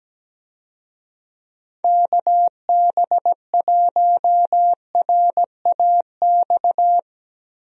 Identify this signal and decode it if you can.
Morse code, heard as a keyed tone, about 17 wpm, 700 Hz. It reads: KB1RAX
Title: USA Call Signs - 6 Characters - Repeated 2x - 17wpm